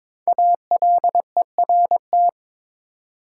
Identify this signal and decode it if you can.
Morse code, heard as a keyed tone, about 22 wpm, 700 Hz. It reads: ALERT
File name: 5 Letter Words - First 3 Letters Given - 22wpm